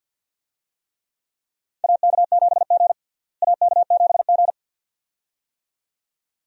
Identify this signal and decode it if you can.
Morse code, heard as a keyed tone, about 53 wpm, 700 Hz. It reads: AK7G AK7G
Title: USA Call Signs - 4 Characters - Repeated 2x - 50wpm